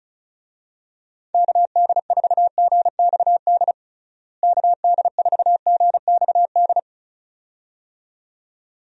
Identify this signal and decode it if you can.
Morse code, heard as a keyed tone, about 34 wpm, 700 Hz. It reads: KD4GXD KD4GXD
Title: USA Call Signs - 6 Characters - Repeated 2x - 35wpm